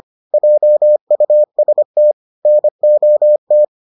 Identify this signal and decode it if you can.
Morse code, heard as a keyed tone, about 25 wpm, 600 Hz. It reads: JUST NOT